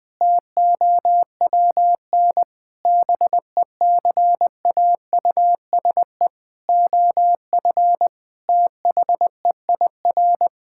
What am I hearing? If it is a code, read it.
Morse code, heard as a keyed tone, about 20 wpm, 700 Hz. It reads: TOWN BECAUSE OF THEIR
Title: Sentences from Top 1000 Words - Review Part 5 - 20wpm